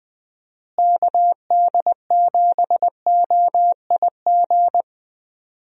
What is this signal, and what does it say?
Morse code, heard as a keyed tone, about 20 wpm, 700 Hz. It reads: KD7OIG
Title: USA Call Signs - 6 Characters - Repeated 3x - 20wpm